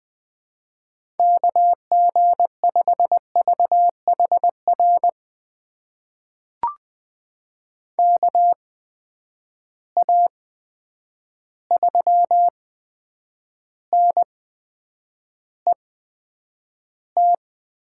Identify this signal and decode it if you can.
Morse code, heard as a keyed tone, about 20 wpm, spaced as for 9 wpm, 700 Hz. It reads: KG5VHR KA3NET
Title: USA Call Signs - 6 Characters - 8x Character Spacing - 20wpm